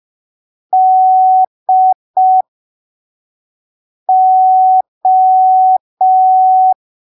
Morse code, heard as a keyed tone, about 5 wpm, spaced as for 3 wpm, 750 Hz. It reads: DO
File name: arquivos / texto / texto 23 05ppm